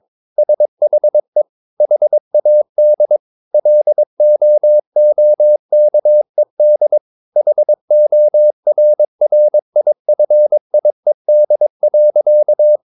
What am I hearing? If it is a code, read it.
Morse code, heard as a keyed tone, about 22 wpm, 600 Hz. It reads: SHE HAD LOOKED HORRIFIED.
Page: CW med Gnister